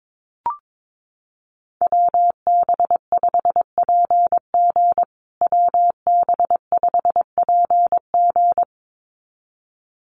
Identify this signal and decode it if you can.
Morse code, heard as a keyed tone, about 22 wpm, 700 Hz. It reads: WB5PG WB5PG
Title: USA Call Signs Repeated 2x 22wpm